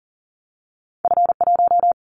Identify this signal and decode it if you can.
Morse code, heard as a keyed tone, about 38 wpm, 700 Hz. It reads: F1
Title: Callsign Prefixes for Top 25 DXCC - 40wpm